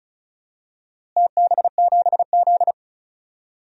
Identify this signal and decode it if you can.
Morse code, heard as a keyed tone, about 34 wpm, 700 Hz. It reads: TB7Z
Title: DXCC-100 Call Signs - 4 Characters - Repeated 2x - 35wpm